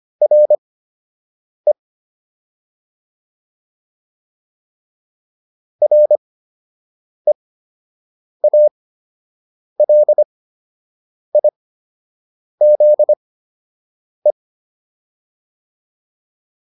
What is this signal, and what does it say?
Morse code, heard as a keyed tone, about 25 wpm, spaced as for 7 wpm, 600 Hz. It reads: RE REALIZE